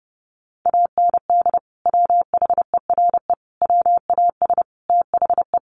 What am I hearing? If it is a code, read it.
Morse code, heard as a keyed tone, about 30 wpm, 700 Hz. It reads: AND WHERE WAS THE
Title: Sentences from Top 1500 Words 30wpm